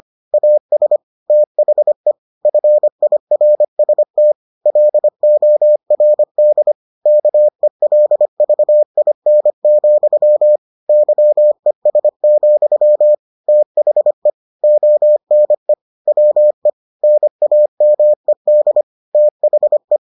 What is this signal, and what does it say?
Morse code, heard as a keyed tone, about 25 wpm, 600 Hz. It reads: AS THE FIRST LORD KELVIN, YES, THE ONE WE NAMED THE